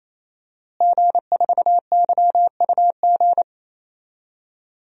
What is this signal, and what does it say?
Morse code, heard as a keyed tone, about 27 wpm, 700 Hz. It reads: G4YUG
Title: DXCC-100 Call Signs - 5 Characters - Repeated 3x - 28wpm